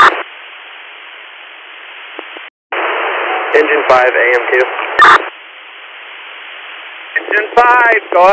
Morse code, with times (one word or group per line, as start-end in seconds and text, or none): none